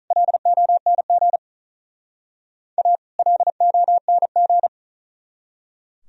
Morse code, heard as a keyed tone, about 44 wpm, 700 Hz. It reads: LONG A LONG